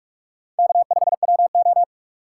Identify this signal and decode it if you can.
Morse code, heard as a keyed tone, about 46 wpm, 700 Hz. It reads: K5WO
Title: USA Call Signs - 4 Characters - 3x Character Spacing - 45wpm